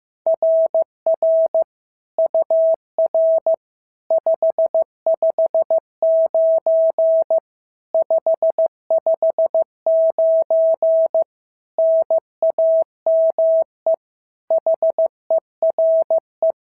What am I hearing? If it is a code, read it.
Morse code, heard as a keyed tone, about 14 wpm, 650 Hz. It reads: RR UR 559 559 NAME HERE
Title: SOTA QSO Practice 15wpm